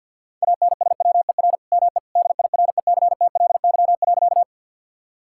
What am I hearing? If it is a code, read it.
Morse code, heard as a keyed tone, about 50 wpm, 690 Hz. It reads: ANSWER ME DIRECTLY.
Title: Sentences from Top 1000 Words - Review Part 5 - 50wpm